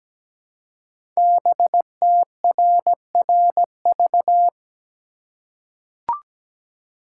Morse code, heard as a keyed tone, about 17 wpm, 700 Hz. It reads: BTRRV
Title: Code Groups - 5 Letters - 17wpm